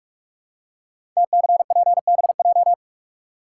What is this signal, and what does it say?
Morse code, heard as a keyed tone, about 44 wpm, 690 Hz. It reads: TCPBJ
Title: Code Groups - 5 Letters - 45wpm